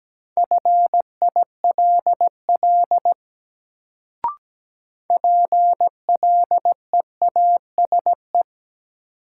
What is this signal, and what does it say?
Morse code, heard as a keyed tone, about 17 wpm, 700 Hz. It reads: FILL PLEASE
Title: Sentences from Top 1500 Words - New Word Drill Alt - 17wpm